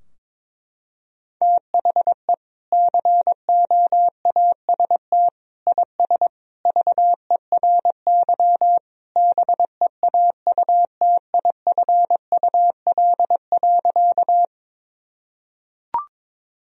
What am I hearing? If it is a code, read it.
Morse code, heard as a keyed tone, about 22 wpm, 700 Hz. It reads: THE COAST IS VERY BEAUTIFUL.
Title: Sentences from Top 1000 Words - Review Part 4 - 22wpm